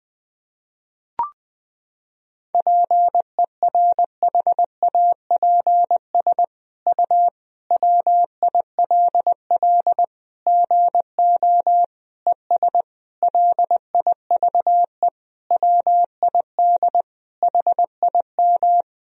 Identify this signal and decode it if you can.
Morse code, heard as a keyed tone, about 20 wpm, 700 Hz. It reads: PERHAPS U WILL GO ES LIVE WID HIM
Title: Sentences from Top 500 Words and Abbreviations 20wpm